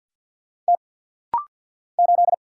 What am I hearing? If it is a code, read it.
Morse code, heard as a keyed tone, about 49 wpm, 700 Hz. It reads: T 8
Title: Single Letter-Number Alt - Rapid-Fire 50wpm